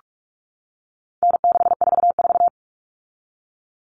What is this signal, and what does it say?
Morse code, heard as a keyed tone, about 44 wpm, 700 Hz. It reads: N644
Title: Common Cut Numbers - 4-Digit 45wpm